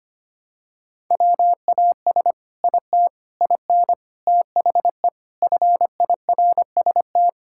Morse code, heard as a keyed tone, about 25 wpm, 700 Hz. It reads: WAS IT IN THE FIRST
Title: Sentences from Top 1000 Words - Review Part 5 - 25wpm